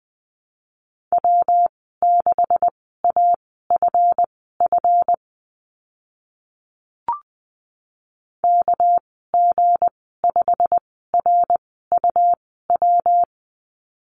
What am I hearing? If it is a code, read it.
Morse code, heard as a keyed tone, about 20 wpm, spaced as for 14 wpm, 700 Hz. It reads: W6AFF KG5RUW
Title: 219 Lesson 35g - Number 0 - Call Signs 20wpm